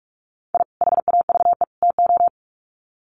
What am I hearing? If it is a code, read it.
Morse code, heard as a keyed tone, about 44 wpm, 700 Hz. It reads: I HAVE TO
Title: Sets of 3 Words 45wpm